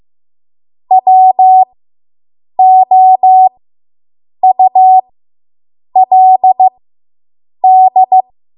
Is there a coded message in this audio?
Yes: Morse code from 0.9 to 8.2 s, heard as a keyed tone, about 15 wpm, spaced as for 7 wpm, 750 Hz. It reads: WOULD